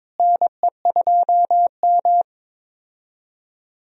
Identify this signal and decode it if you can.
Morse code, heard as a keyed tone, about 21 wpm, 700 Hz. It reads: NE2M
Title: USA Call Signs - 4 Characters - Repeated 2x - 22wpm